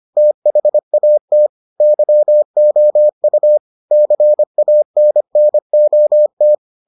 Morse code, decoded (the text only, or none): THAT YOU CANNOT